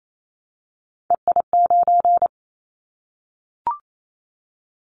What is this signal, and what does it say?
Morse code, heard as a keyed tone, about 28 wpm, 700 Hz. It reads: EI9
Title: Callsign Prefixes for Top 50 DXCC - 28wpm